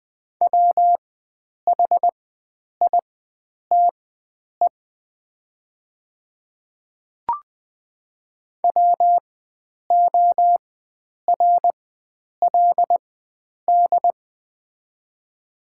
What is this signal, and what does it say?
Morse code, heard as a keyed tone, about 21 wpm, spaced as for 9 wpm, 700 Hz. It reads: WHITE WORLD
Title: 4x Character Spacing - Top 500 Words - 20wpm